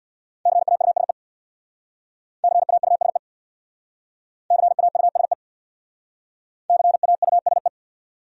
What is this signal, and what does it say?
Morse code, heard as a keyed tone, about 61 wpm, 700 Hz. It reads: CAUSE CAUSE CAUSE CAUSE
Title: Speed-Racing Top 500 Words 50wpm